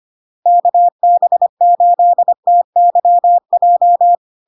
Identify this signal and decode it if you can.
Morse code, heard as a keyed tone, about 25 wpm, 700 Hz. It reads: KB8TYJ
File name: LARK-FD-2022-Log-in-CW-25wpm-700-Hz